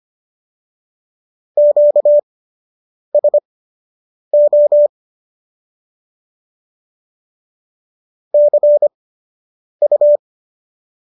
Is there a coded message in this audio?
Yes: Morse code from 1.6 to 10.2 s, heard as a keyed tone, about 25 wpm, spaced as for 8 wpm, 590 Hz. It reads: QSO CU